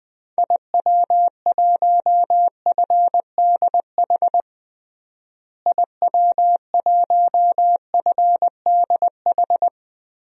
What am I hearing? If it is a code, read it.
Morse code, heard as a keyed tone, about 20 wpm, 700 Hz. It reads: IW1FDH IW1FDH
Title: DXCC-100 Call Signs - 6 Characters - Repeated 3x - 20wpm